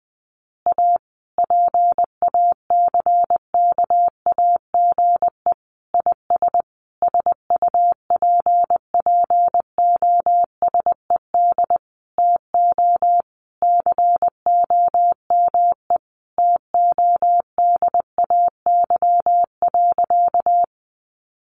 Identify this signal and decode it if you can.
Morse code, heard as a keyed tone, about 20 wpm, 700 Hz. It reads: A PACKAGE IS SUPPOSED TO COME TODAY.
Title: Additional Easy Phrases 20wpm